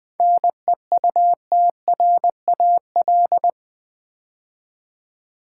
Morse code, heard as a keyed tone, about 21 wpm, 700 Hz. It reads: NEUTRAL
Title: 7 Letter Words - First 3 Letters Given - 20wpm